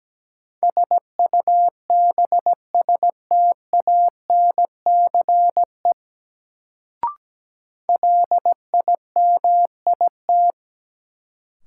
Morse code, heard as a keyed tone, about 17 wpm, 700 Hz. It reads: SUBSTANCE LIMIT
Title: Sentences from Top 1500 Words - New Word Drill Alt - 17wpm